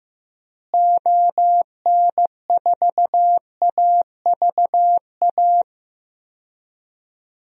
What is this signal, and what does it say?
Morse code, heard as a keyed tone, about 15 wpm, 700 Hz. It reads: ON4AVA
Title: DXCC-100 Call Signs - 6 Characters - Repeated 3x - 15wpm